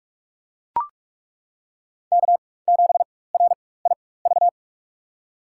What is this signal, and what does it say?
Morse code, heard as a keyed tone, about 42 wpm, spaced as for 20 wpm, 700 Hz. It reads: K7RIV
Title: USA Call Signs - 5 Characters - 4x Character Spacing - 45wpm